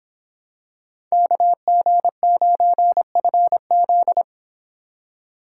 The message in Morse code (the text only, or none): KG9FZ